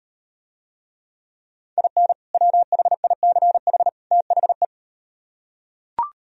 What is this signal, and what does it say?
Morse code, heard as a keyed tone, about 38 wpm, 700 Hz. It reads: IN WHICH THE